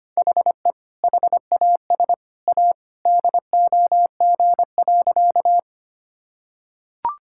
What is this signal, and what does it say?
Morse code, heard as a keyed tone, about 25 wpm, 700 Hz. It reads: HE HAS A DOG.
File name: practice / 25wpm / Sentences with 2-Word Phrases - 25wpm